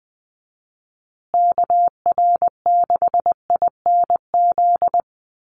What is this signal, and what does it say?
Morse code, heard as a keyed tone, about 20 wpm, 700 Hz. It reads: KR6INZ
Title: USA Call Signs Repeated 2x 20wpm